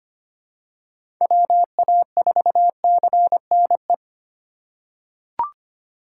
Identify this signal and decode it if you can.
Morse code, heard as a keyed tone, about 25 wpm, 700 Hz. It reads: WA4CNE
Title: USA Call Signs - 4x Character Spacing - 25wpm